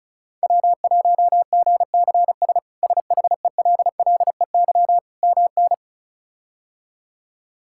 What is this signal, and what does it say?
Morse code, heard as a keyed tone, about 36 wpm, 700 Hz. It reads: W1GCS SHELLEY MN
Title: K1USN SST QSO Practice - 35wpm